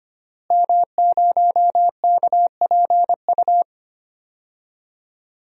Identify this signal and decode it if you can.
Morse code, heard as a keyed tone, about 25 wpm, 700 Hz. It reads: M0KPU
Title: DXCC-100 Call Signs - 5 Characters - Repeated 3x - 25wpm